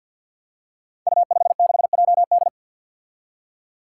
Morse code, heard as a keyed tone, about 53 wpm, 700 Hz. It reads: U56JD
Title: Five Challenging Letter-Number Combinations 50wpm